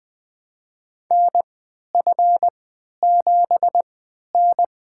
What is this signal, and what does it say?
Morse code, heard as a keyed tone, about 20 wpm, spaced as for 11 wpm, 700 Hz. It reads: NF7N
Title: USA Call Signs - 4 Characters - 3x Character Spacing - 20wpm